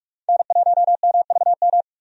Morse code, heard as a keyed tone, about 45 wpm, 700 Hz. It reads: N1MVM
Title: USA Call Signs - 5 Characters - 45wpm